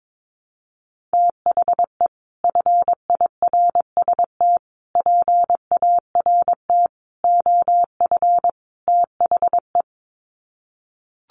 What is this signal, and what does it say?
Morse code, heard as a keyed tone, about 22 wpm, 700 Hz. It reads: THE FIRST PART OF THE
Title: Sets of 5 Words Encore 22wpm